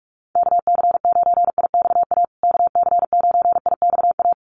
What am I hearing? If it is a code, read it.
Morse code, heard as a keyed tone, about 44 wpm, 700 Hz. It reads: KC9IXA KC9IXA
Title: USA Call Signs Repeated 3x 45wpm